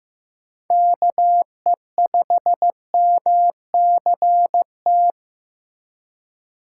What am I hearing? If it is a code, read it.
Morse code, heard as a keyed tone, about 15 wpm, 700 Hz. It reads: KE5MCT
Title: USA Call Signs - 6 Characters - 3x Character Spacing - 15wpm